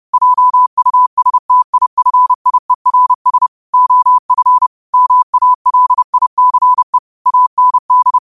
Morse code, heard as a keyed tone, about 30 wpm, 1000 Hz. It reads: JUSTIFIERS OF MALICE AND
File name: qotdfiles / 30wpm